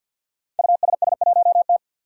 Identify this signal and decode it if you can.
Morse code, heard as a keyed tone, about 51 wpm, 700 Hz. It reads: USS1T
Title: Five Challenging Letter-Number Combinations 50wpm